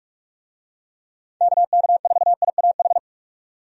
Morse code, heard as a keyed tone, about 44 wpm, 700 Hz. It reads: KK4IAH
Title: USA Call Signs - 6 Characters - 45wpm